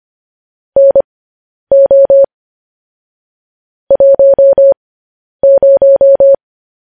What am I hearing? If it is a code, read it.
Morse code, heard as a keyed tone, about 25 wpm, spaced as for 10 wpm, 550 Hz. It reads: NO 10
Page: news-headlines-in-morse-code-at-10-wpm